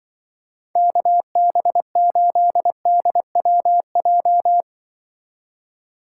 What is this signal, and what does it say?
Morse code, heard as a keyed tone, about 23 wpm, 700 Hz. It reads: KB8DWJ
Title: Speed Racing - USA 2x3 Call Signs 20wpm